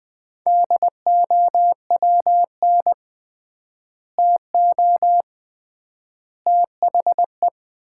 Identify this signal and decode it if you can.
Morse code, heard as a keyed tone, about 20 wpm, 700 Hz. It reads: DOWN TO THE
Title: 3x Word-Spacing - Sets of 3 Words - 20wpm